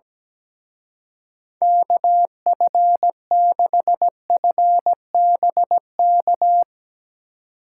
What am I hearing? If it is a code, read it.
Morse code, heard as a keyed tone, about 18 wpm, 700 Hz. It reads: KF6FBK
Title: Speed Racing - USA 2x3 Call Signs 17wpm